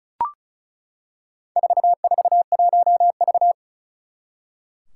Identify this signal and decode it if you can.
Morse code, heard as a keyed tone, about 36 wpm, 700 Hz. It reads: 441V
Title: Four Challenging Letter-Number Combinations 35wpm